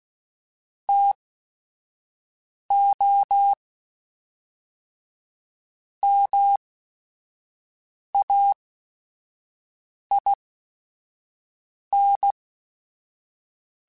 Morse code, heard as a keyed tone, about 16 wpm, spaced as for 5 wpm, 800 Hz. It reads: TO MAIN
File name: Technical Subjects / Technical Subjects-mp3s / XC @5